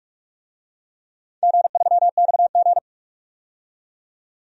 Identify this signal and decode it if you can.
Morse code, heard as a keyed tone, about 46 wpm, 700 Hz. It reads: G3XG
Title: DXCC-100 Call Signs - 4 Characters - 45wpm